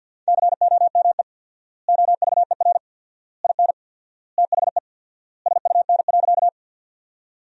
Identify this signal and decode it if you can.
Morse code, heard as a keyed tone, about 50 wpm, 700 Hz. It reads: COME OVER IN THE SUN.
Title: 4x Word-Spacing - Sentences from Top 300 Words - 50wpm